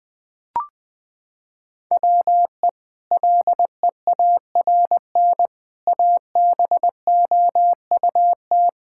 Morse code, heard as a keyed tone, about 20 wpm, 700 Hz. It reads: WE LEARN ABOUT
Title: Sentences from Top 1000 Words - Review Part 5 - 20wpm